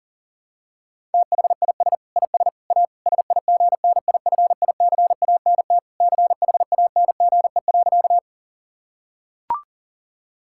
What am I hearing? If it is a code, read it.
Morse code, heard as a keyed tone, about 40 wpm, 700 Hz. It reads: THIS IS A SIGNIFICANT CHANGE.
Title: Sentences from Top 1000 Words - Review Part 6 - 40wpm